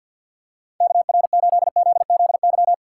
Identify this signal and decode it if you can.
Morse code, heard as a keyed tone, about 52 wpm, 700 Hz. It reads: KR8ZZY
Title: USA Call Signs - 6 Characters - 6x Character Spacing - 50wpm